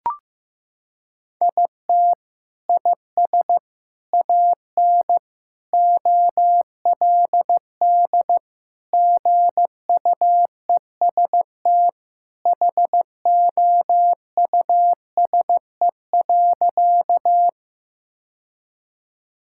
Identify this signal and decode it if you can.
Morse code, heard as a keyed tone, about 15 wpm, 700 Hz. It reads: IT IS AN OLD GUEST HOUSE.
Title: Sentences from Top 1600 Words 15wpm